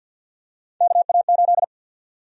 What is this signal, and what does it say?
Morse code, heard as a keyed tone, about 52 wpm, 700 Hz. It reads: KA8
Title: USA Call Sign Prefixes - 50wpm